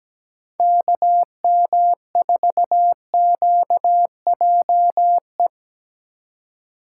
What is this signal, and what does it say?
Morse code, heard as a keyed tone, about 17 wpm, 700 Hz. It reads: KM4QJE
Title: USA Call Signs - 3x Character Spacing - 17wpm